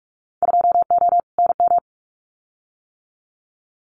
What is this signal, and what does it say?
Morse code, heard as a keyed tone, about 44 wpm, 700 Hz. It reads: 2O NM
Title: Winter Field Day Exchanges 45wpm